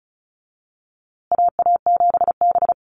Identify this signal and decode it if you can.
Morse code, heard as a keyed tone, about 34 wpm, 700 Hz. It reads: AA7B
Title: USA Call Signs Repeated 2x 35wpm